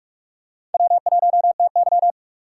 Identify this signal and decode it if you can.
Morse code, heard as a keyed tone, about 46 wpm, 700 Hz. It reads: W1TY
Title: USA Call Signs - 3x Character Spacing - 45wpm